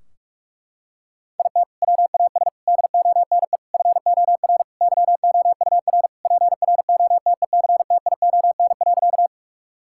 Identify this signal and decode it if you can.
Morse code, heard as a keyed tone, about 46 wpm, 700 Hz. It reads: IT WAS DONE FOR YOUR PROTECTION.
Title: Sentences from Top 1600 Words 45wpm